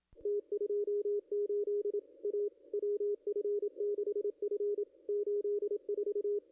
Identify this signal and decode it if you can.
Morse code, heard as a keyed tone, about 27 wpm, 1190 Hz, with radio noise. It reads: T28 A WF6F 84
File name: contests / ss-cw-k5tr-op-n5rz -0200